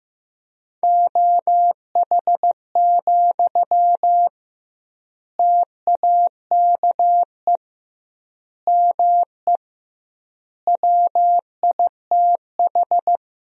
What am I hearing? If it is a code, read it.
Morse code, heard as a keyed tone, about 15 wpm, 700 Hz. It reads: OH, TAKE ME WITH
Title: 2x Word-Spacing - Sentences from Top 500 Words - 15wpm